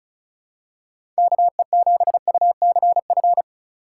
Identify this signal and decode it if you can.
Morse code, heard as a keyed tone, about 34 wpm, 700 Hz. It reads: KE7UCF
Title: USA Call Signs - 2x Character Spacing - 35wpm